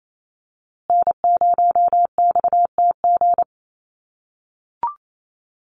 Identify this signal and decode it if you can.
Morse code, heard as a keyed tone, about 28 wpm, 700 Hz. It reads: N0XTG
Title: USA Call Signs Repeated 3x 28wpm